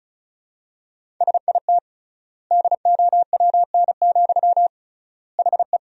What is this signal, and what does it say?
Morse code, heard as a keyed tone, about 36 wpm, 700 Hz. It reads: SIT DOWN, HE
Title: 3x Word-Spacing - Sentences from Top 500 Words - 35wpm